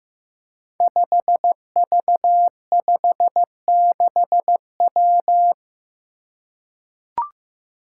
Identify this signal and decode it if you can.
Morse code, heard as a keyed tone, about 15 wpm, 700 Hz. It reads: 5V56W
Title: Five Challenging Letter-Number Combinations 15wpm